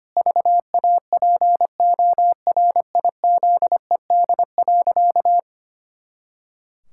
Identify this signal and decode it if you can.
Morse code, heard as a keyed tone, about 25 wpm, 700 Hz. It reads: VAPORIZED.
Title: Pangram Challenge v2 - 25wpm